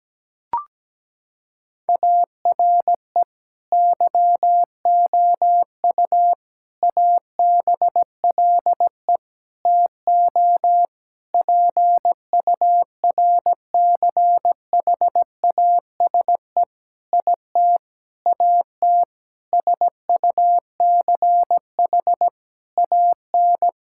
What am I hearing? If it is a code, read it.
Morse code, heard as a keyed tone, about 17 wpm, 700 Hz. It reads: ARE YOU ABLE TO PURCHASE IT AT SUCH AN
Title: Sentences from Top 1000 Words - Review Part 5 - 17wpm